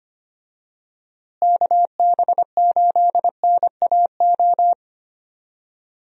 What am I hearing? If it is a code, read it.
Morse code, heard as a keyed tone, about 25 wpm, 700 Hz. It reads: KB8NAO
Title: USA Call Signs - 6 Characters - Repeated 2x - 25wpm